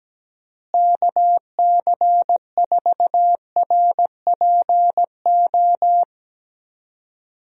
Morse code, heard as a keyed tone, about 17 wpm, 700 Hz. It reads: KC4RPO
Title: USA Call Signs - 6 Characters - Repeated 3x - 17wpm